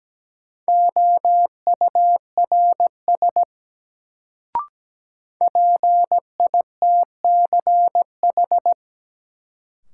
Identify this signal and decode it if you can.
Morse code, heard as a keyed tone, about 17 wpm, 700 Hz. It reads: OURS PITCH
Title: Sentences from Top 1300 Words - New Word Drill Alt - 17wpm